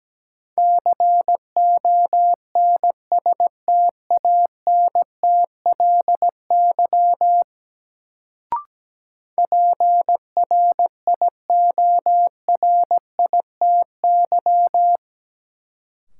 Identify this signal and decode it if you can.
Morse code, heard as a keyed tone, about 17 wpm, 700 Hz. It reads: CONSTANTLY PRIORITY
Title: Sentences from Top 1600 Words - New Word Drill Alt - 17wpm